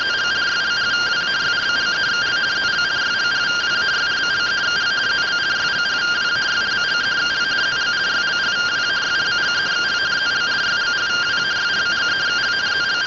Two-tone teleprinter signal, about 170 Hz shift.